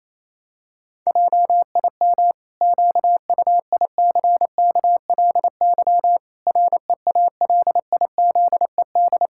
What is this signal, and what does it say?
Morse code, heard as a keyed tone, about 28 wpm, 700 Hz. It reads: JIM QUICKLY REALIZED
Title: Pangram Challenge v2 - 28wpm